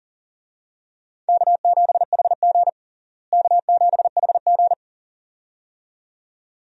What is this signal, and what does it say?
Morse code, heard as a keyed tone, about 40 wpm, 700 Hz. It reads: K7HG K7HG
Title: USA Call Signs - 4 Characters - Repeated 2x - 40wpm